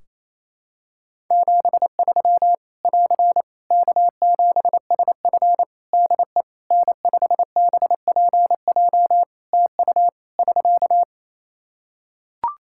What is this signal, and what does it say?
Morse code, heard as a keyed tone, about 28 wpm, 700 Hz. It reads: 73 <AR> K7SF DE N5BPJ TU <SK>
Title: Practice QSO - Exchange 4 - Ending - 28wpm